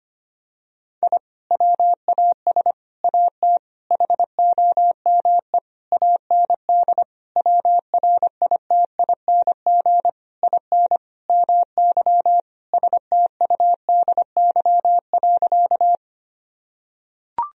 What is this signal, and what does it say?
Morse code, heard as a keyed tone, about 25 wpm, 700 Hz. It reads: I WAS AT HOME AND WRITING IN MY STUDY.
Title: Sentences from Top 1000 Words - Review Part 1 - 25wpm